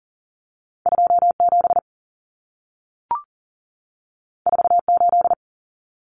Morse code, heard as a keyed tone, about 40 wpm, 700 Hz. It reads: 27 48
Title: Common Cut Numbers - 2-Digit 40wpm